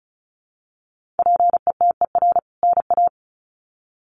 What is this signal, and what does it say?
Morse code, heard as a keyed tone, about 34 wpm, 700 Hz. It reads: PETER NA
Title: CWT Exchange 35wpm